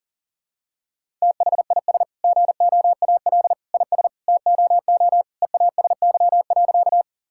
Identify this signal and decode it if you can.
Morse code, heard as a keyed tone, about 40 wpm, 700 Hz. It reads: THIS GOAL IS TOO EASY.
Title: Sentences from Top 1000 Words - Review Part 5 - 40wpm